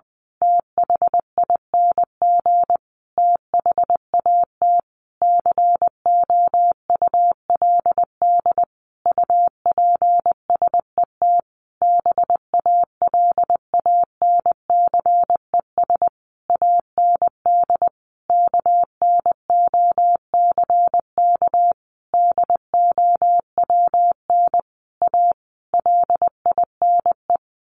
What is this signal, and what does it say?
Morse code, heard as a keyed tone, about 20 wpm, 700 Hz. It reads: THING THAT COULD UPSET BALANCES AND KNOCK DOWN A LINE
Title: A Sound Of Thunder 20wpm